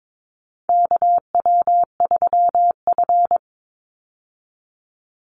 Morse code, heard as a keyed tone, about 22 wpm, 700 Hz. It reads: KW3F
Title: Call Signs - 22wpm